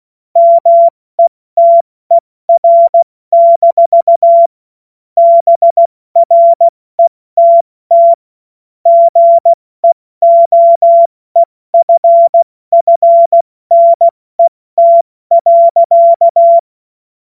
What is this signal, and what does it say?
Morse code, heard as a keyed tone, about 16 wpm, 680 Hz. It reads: METER- BRETT GEOEFFNET.
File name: Morsekurs / Witze / Witz 21. September 2021, 70 F80